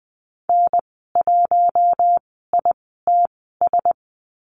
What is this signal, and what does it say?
Morse code, heard as a keyed tone, about 20 wpm, spaced as for 14 wpm, 700 Hz. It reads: N1ITS
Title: 048 Lesson 9g - Number 4 - Call Signs 20wpm